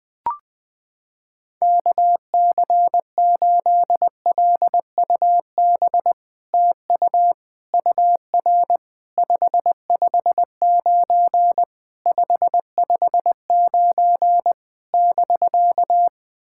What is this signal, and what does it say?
Morse code, heard as a keyed tone, about 20 wpm, 700 Hz. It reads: KC8LUB TU UR 559 559 <BK>
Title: POTA QSO Practice - 20wpm